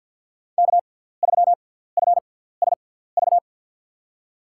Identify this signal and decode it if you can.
Morse code, heard as a keyed tone, about 51 wpm, spaced as for 17 wpm, 700 Hz. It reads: K3FSV
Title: USA Call Signs - 6x Character Spacing - 50wpm